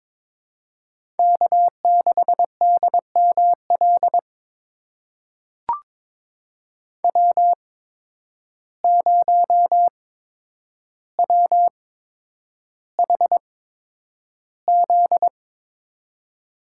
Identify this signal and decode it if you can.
Morse code, heard as a keyed tone, about 23 wpm, spaced as for 9 wpm, 700 Hz. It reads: K6DML W0WHZ
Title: USA Call Signs - 8x Character Spacing - 22wpm